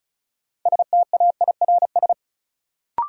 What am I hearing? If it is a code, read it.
Morse code, heard as a keyed tone, about 36 wpm, 700 Hz. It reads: STAIRS
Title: Sets of 5 Words Encore - New Word Drill - 35wpm